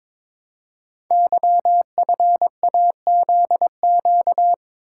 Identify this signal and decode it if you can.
Morse code, heard as a keyed tone, about 23 wpm, 700 Hz. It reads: YFAZQ
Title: Code Groups - 5 Letter:Numbers - 22wpm